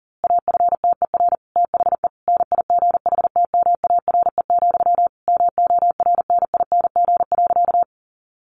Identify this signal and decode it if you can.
Morse code, heard as a keyed tone, about 37 wpm, 700 Hz. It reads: AFTER THE NIGHTMARE, MORNING.
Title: A Sound Of Thunder 40wpm